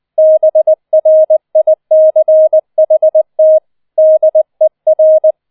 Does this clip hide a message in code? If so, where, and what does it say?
Yes: Morse code from 0.2 to 5.3 s, heard as a keyed tone, about 20 wpm, 630 Hz. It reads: BRICHT DER